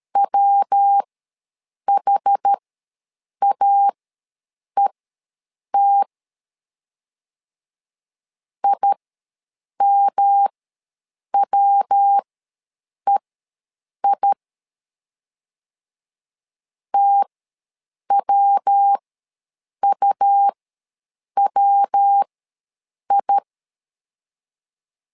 Morse code, heard as a keyed tone, about 13 wpm, spaced as for 7 wpm, 800 Hz. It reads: WHAET IMWEI TWUWI
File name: LEZIONI / 3 MWRO / 3 MW ws 12wpm 1816